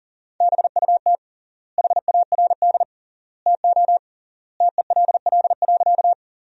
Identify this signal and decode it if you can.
Morse code, heard as a keyed tone, about 40 wpm, 700 Hz. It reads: BUT HARD TO TELL.
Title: 3x Word-Spacing - Sentences from Top 300 Words - 40wpm